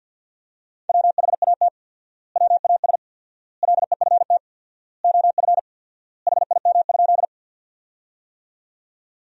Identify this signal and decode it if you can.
Morse code, heard as a keyed tone, about 50 wpm, 700 Hz. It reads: WHAT WAS LEFT OF HIM?
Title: 4x Word-Spacing - Sentences from Top 300 Words - 50wpm